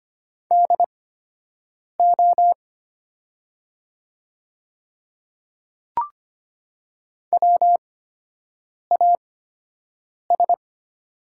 Morse code, heard as a keyed tone, about 25 wpm, spaced as for 7 wpm, 690 Hz. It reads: DO WAS
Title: 8x Character Spacing - Top 200 Words - 25wpm